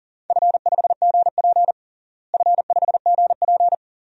Morse code, heard as a keyed tone, about 40 wpm, 700 Hz. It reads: F5GP F5GP
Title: DXCC-100 Call Signs - 4 Characters - Repeated 2x - 40wpm